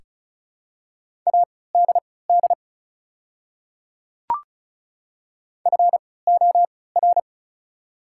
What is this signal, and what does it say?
Morse code, heard as a keyed tone, about 36 wpm, spaced as for 20 wpm, 700 Hz. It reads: ADD FOR
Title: 3x Character Spacing - Top 300 Words - 35wpm